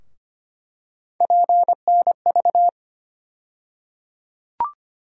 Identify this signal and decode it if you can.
Morse code, heard as a keyed tone, about 25 wpm, 700 Hz. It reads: PNV
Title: USA Call Sign Postfixes - 3 Letters 25wpm